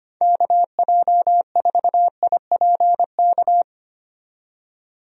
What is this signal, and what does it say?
Morse code, heard as a keyed tone, about 25 wpm, 700 Hz. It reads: KJ4IPK
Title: USA Call Signs - 6 Characters - 25wpm